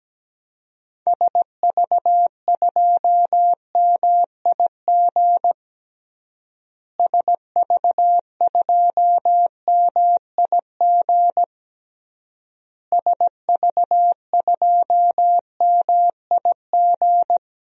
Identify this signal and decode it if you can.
Morse code, heard as a keyed tone, about 17 wpm, 700 Hz. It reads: SV2MIG SV2MIG SV2MIG
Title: DXCC-100 Call Signs - 6 Characters - Repeated 3x - 17wpm